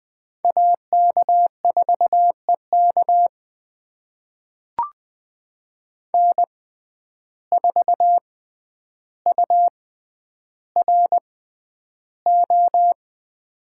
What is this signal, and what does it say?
Morse code, heard as a keyed tone, about 20 wpm, spaced as for 10 wpm, 700 Hz. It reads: AK4EK N4URO
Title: USA Call Signs - 5 Characters - 6x Character Spacing - 20wpm